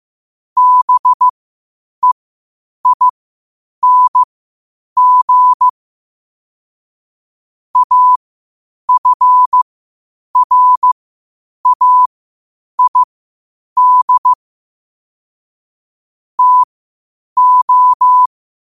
Morse code, heard as a keyed tone, about 15 wpm, spaced as for 8 wpm, 1000 Hz. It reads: BEING AFRAID TO
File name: qotdfiles / 8wpm